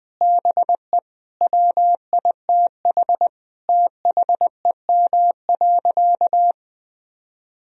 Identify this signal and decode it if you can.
Morse code, heard as a keyed tone, about 20 wpm, 700 Hz. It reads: BE WITH THEM.
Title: Sentences from Top 1000 Words - Review Part 4 - 20wpm